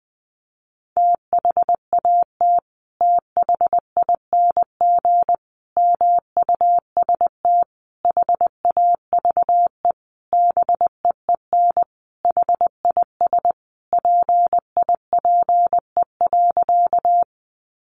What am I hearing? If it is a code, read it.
Morse code, heard as a keyed tone, about 20 wpm, 700 Hz. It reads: THAT THING MUST HAVE BEEN HIS PIPE.
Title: Sentences from Top 1500 Words 20wpm